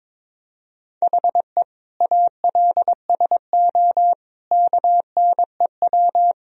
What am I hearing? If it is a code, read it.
Morse code, heard as a keyed tone, about 22 wpm, 700 Hz. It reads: HE ALSO KNEW
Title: Sentences from Top 1000 Words - Review Part 1 - 22wpm